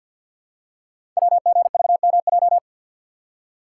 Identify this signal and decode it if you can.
Morse code, heard as a keyed tone, about 50 wpm, 700 Hz. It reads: WGVMJ
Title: Code Groups - 5 Letters - 50wpm